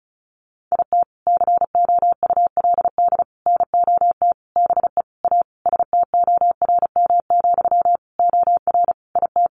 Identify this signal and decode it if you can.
Morse code, heard as a keyed tone, about 34 wpm, 700 Hz. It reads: IT COULD NOT BE A STORM, OR IT
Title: Sentences from Top 1500 Words 35wpm